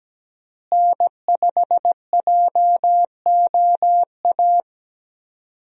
Morse code, heard as a keyed tone, about 17 wpm, 700 Hz. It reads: N5JOA
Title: USA Call Signs - 5 Characters - Repeated 3x - 17wpm